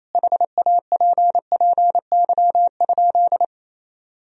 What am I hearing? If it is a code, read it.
Morse code, heard as a keyed tone, about 28 wpm, 700 Hz. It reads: HAPPY?
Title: Sentences from Top 1000 Words - Review Part 2 - 28wpm